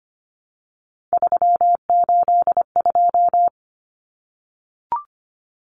Morse code, heard as a keyed tone, about 25 wpm, 700 Hz. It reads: 382
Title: Common Cut Numbers - 3-Digit 25wpm